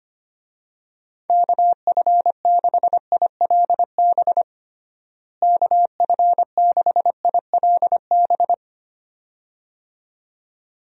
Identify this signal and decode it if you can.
Morse code, heard as a keyed tone, about 25 wpm, 700 Hz. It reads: KF6ILB KF6ILB
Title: USA Call Signs - 6 Characters - Repeated 2x - 25wpm